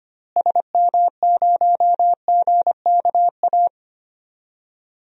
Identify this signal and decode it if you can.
Morse code, heard as a keyed tone, about 25 wpm, 700 Hz. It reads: SM0GKA
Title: DXCC-100 Call Signs - 6 Characters - Repeated 3x - 25wpm